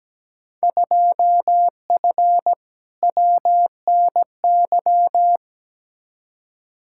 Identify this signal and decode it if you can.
Morse code, heard as a keyed tone, about 17 wpm, 700 Hz. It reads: 2F WNY
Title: Field Day Exchanges Encore - 17wpm